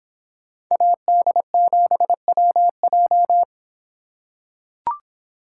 Morse code, heard as a keyed tone, about 25 wpm, 700 Hz. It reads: AD7WJ